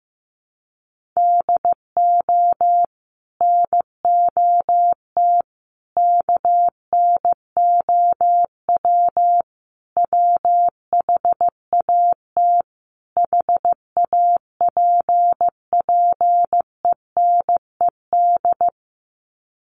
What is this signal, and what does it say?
Morse code, heard as a keyed tone, about 14 wpm, 700 Hz. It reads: DO NOT KNOW WHAT HAPPENED
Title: Sets of 5 Words Encore2 15wpm